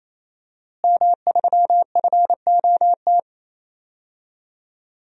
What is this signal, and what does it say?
Morse code, heard as a keyed tone, about 28 wpm, 700 Hz. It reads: M3FOT
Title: DXCC-100 Call Signs - 5 Characters - Repeated 3x - 28wpm